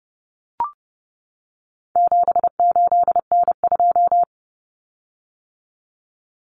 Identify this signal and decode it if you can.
Morse code, heard as a keyed tone, about 30 wpm, 700 Hz. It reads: 78N2
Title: Common Cut Numbers - 4-Digit 30wpm